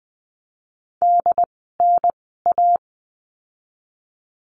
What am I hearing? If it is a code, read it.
Morse code, heard as a keyed tone, about 20 wpm, spaced as for 14 wpm, 700 Hz. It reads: DNA
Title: 068 Lesson 12f - Letter d - Words 20wpm